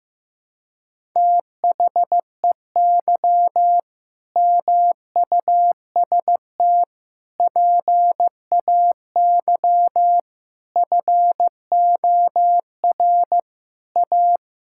Morse code, heard as a keyed tone, about 15 wpm, 700 Hz. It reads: THEY MUST PAY FOR A
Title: Sentences from Top 1000 Words - Review Part 6 - 15wpm